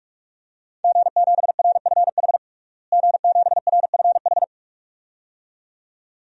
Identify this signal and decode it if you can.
Morse code, heard as a keyed tone, about 46 wpm, 700 Hz. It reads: G7RFH G7RFH
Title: DXCC-100 Call Signs - 5 Characters - Repeated 2x - 45wpm